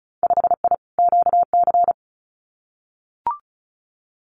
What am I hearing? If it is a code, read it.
Morse code, heard as a keyed tone, about 34 wpm, 700 Hz. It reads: 5I QC
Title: Winter Field Day Exchanges Encore 35wpm